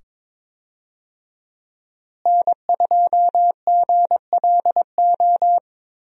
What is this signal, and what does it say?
Morse code, heard as a keyed tone, about 22 wpm, 700 Hz. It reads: N2GLO
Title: USA Call Signs - 5 Characters - Repeated 2x - 22wpm